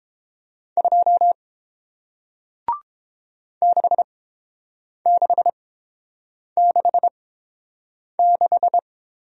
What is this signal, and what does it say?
Morse code, heard as a keyed tone, about 29 wpm, spaced as for 7 wpm, 700 Hz. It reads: 2 6666